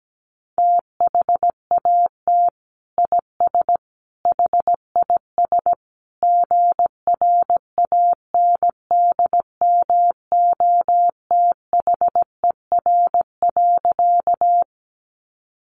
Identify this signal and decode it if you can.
Morse code, heard as a keyed tone, about 17 wpm, 700 Hz. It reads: THAT IS HIS GRANDMOTHER.
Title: Sentences from Top 1200 Words 17wpm